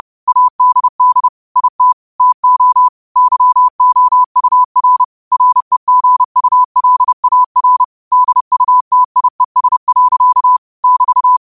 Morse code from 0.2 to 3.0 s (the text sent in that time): ADD IT TO